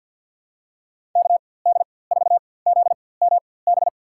Morse code, heard as a keyed tone, about 52 wpm, spaced as for 23 wpm, 700 Hz. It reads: KD4ZMB
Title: USA Call Signs - 6 Characters - 4x Character Spacing - 50wpm